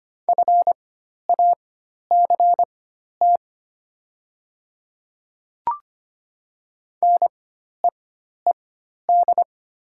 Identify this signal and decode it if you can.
Morse code, heard as a keyed tone, about 25 wpm, spaced as for 12 wpm, 700 Hz. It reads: FACT NEED
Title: 4x Character Spacing - Top 500 Words - 25wpm